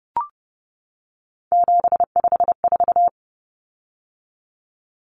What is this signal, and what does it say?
Morse code, heard as a keyed tone, about 30 wpm, 700 Hz. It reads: 754
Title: Common Cut Numbers - 3-Digit 30wpm